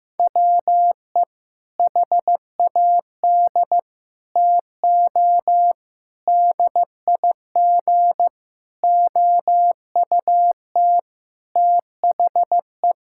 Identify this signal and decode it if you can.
Morse code, heard as a keyed tone, about 15 wpm, 700 Hz. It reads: WE HAD TO DIG OUT THE
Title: Sentences from Top 1600 Words 15wpm